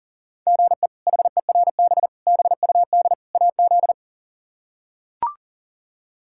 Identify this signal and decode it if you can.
Morse code, heard as a keyed tone, about 40 wpm, 700 Hz. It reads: GE HERB BUD AZ
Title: K1USN SST QSO Practice - 40wpm